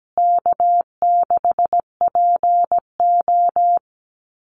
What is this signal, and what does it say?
Morse code, heard as a keyed tone, about 17 wpm, 700 Hz. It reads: K6PO
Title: Call Signs Repeated 2x - 17wpm